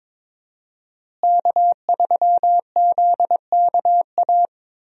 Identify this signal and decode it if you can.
Morse code, heard as a keyed tone, about 21 wpm, 700 Hz. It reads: K3ZKA
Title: USA Call Signs - 5 Characters - Repeated 2x - 22wpm